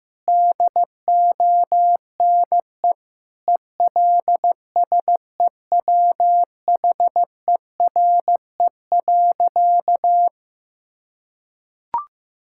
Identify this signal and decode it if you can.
Morse code, heard as a keyed tone, about 15 wpm, 700 Hz. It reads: DONE ELSEWHERE.
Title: Sentences from Top 1600 Words 15wpm